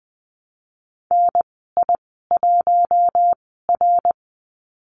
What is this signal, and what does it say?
Morse code, heard as a keyed tone, about 20 wpm, spaced as for 15 wpm, 700 Hz. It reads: NI1R
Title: 055 Lesson 10g - Letter r - Call Signs 20wpm